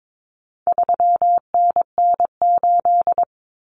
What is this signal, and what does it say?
Morse code, heard as a keyed tone, about 22 wpm, 700 Hz. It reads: 3NN8
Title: Common Cut Numbers - 4-Digit 22wpm